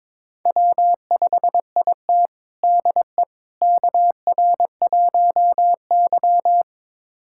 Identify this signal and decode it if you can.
Morse code, heard as a keyed tone, about 22 wpm, 700 Hz. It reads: W5IT DE KR1Y
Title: USA Callsign DE Callsign - 4 Characters - 22wpm